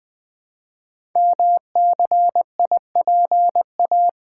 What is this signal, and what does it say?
Morse code, heard as a keyed tone, about 20 wpm, 700 Hz. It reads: MCIPA
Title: Code Groups - 5 Letters - 20wpm